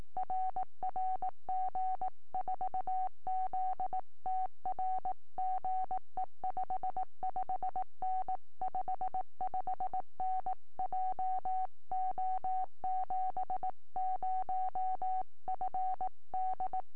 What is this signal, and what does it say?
Morse code, heard as a keyed tone, about 18 wpm, 750 Hz. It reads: RR G4ZTR GE 55N 55N JO70FD
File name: tropo / g4ztr 70cm